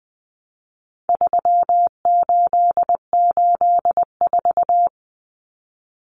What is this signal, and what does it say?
Morse code, heard as a keyed tone, about 20 wpm, 700 Hz. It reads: 3884
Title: Common Cut Numbers - 4-Digit 20wpm